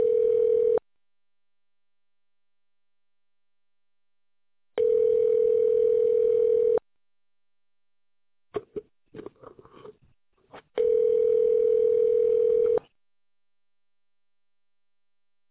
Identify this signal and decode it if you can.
Ringback tone (North America), 440 + 480 Hz.